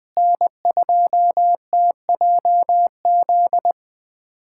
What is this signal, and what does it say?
Morse code, heard as a keyed tone, about 21 wpm, 700 Hz. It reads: N2TJZ
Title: USA Callsign DE Callsign 20wpm